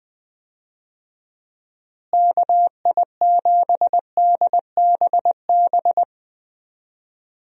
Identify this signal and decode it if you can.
Morse code, heard as a keyed tone, about 21 wpm, 700 Hz. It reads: KI7DBB
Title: USA Call Signs - 6 Characters - Repeated 2x - 20wpm